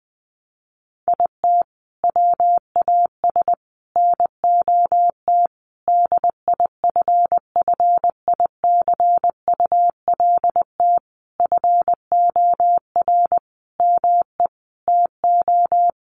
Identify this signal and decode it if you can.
Morse code, heard as a keyed tone, about 20 wpm, 700 Hz. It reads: IT WAS NOT DIFFICULT FOR ME TO
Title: Sentences from Top 1200 Words 20wpm